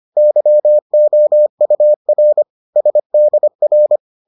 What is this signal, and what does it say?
Morse code, heard as a keyed tone, about 25 wpm, 600 Hz. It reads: YOUR SDR